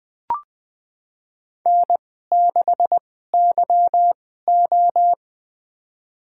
Morse code, heard as a keyed tone, about 20 wpm, spaced as for 14 wpm, 700 Hz. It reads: N6YO
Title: USA Call Signs - 4 Characters - 2x Character Spacing - 20wpm